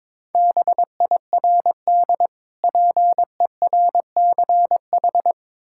Morse code, heard as a keyed tone, about 22 wpm, 700 Hz. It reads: BIRD PERCH
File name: Difficult Words with Hints - 22wpm